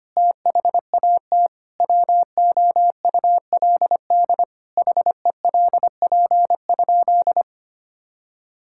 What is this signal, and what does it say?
Morse code, heard as a keyed tone, about 25 wpm, 700 Hz. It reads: THAT WOULD HELP?
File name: practice / 25wpm / Sentences with 9-Word Phrases - 25wpm